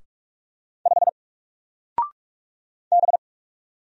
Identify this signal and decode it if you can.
Morse code, heard as a keyed tone, about 44 wpm, 700 Hz. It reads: 5 B
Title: Single Letter-Number 45wpm